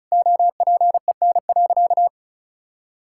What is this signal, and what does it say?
Morse code, heard as a keyed tone, about 36 wpm, 700 Hz. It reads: OPEN.
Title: Sentences from Top 1600 Words 35wpm